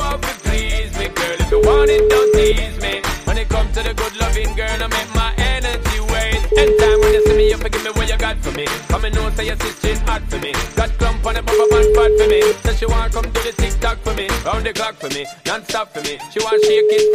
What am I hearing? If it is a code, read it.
Ringback tone (Europe, 425 Hz), with noise.